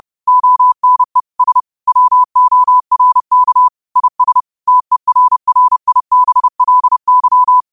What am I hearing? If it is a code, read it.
Morse code, heard as a keyed tone, about 30 wpm, 1000 Hz. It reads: ONE S WORK IS TERRIBLY
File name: qotdfiles / 30wpm